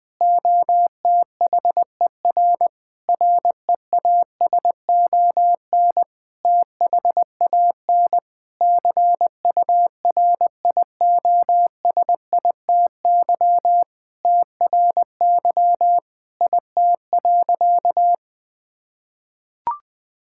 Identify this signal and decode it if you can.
Morse code, heard as a keyed tone, about 20 wpm, 700 Hz. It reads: OTHER REASON THAN CURIOSITY TRY IT.
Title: Sentences with 6-Word Phrases - Spoken Beginning - 20wpm